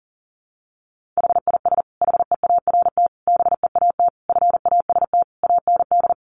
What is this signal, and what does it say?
Morse code, heard as a keyed tone, about 40 wpm, 700 Hz. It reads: HIS HEART BEAT FAST AND
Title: Sentences from Top 1300 Words 40wpm